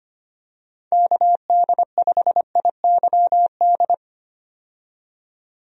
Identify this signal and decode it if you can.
Morse code, heard as a keyed tone, about 25 wpm, 700 Hz. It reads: KD5IYD
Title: USA Call Signs - 6 Characters - 25wpm